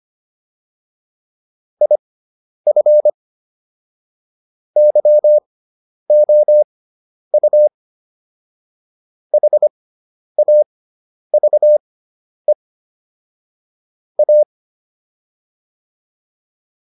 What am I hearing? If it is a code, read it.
Morse code, heard as a keyed tone, about 25 wpm, spaced as for 10 wpm, 600 Hz. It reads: IF YOU HAVE A